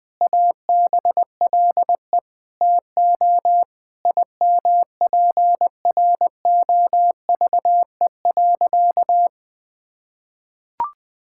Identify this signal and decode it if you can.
Morse code, heard as a keyed tone, about 20 wpm, 700 Hz. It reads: ABLE TO IMPROVE.
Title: Sentences with 4-Word Phrases - Spoken Beginning - 20wpm